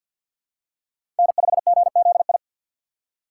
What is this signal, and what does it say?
Morse code, heard as a keyed tone, about 48 wpm, 700 Hz. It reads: N5GZI
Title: USA Call Signs - 2x Character Spacing - 50wpm